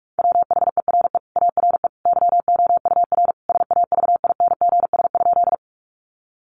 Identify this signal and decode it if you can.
Morse code, heard as a keyed tone, about 44 wpm, 700 Hz. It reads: WHERE ARE YOUR SAVINGS?
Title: Sentences from Top 1200 Words 45wpm